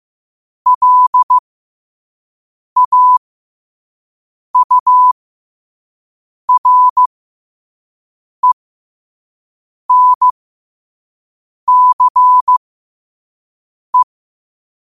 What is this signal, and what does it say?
Morse code, heard as a keyed tone, about 15 wpm, spaced as for 5 wpm, 1000 Hz. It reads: LAURENCE
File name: qotdfiles / 5wpm